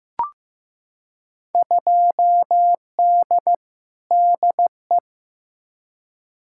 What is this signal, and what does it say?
Morse code, heard as a keyed tone, about 15 wpm, 700 Hz. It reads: 2D DE
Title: Field Day Exchanges Encore - 15wpm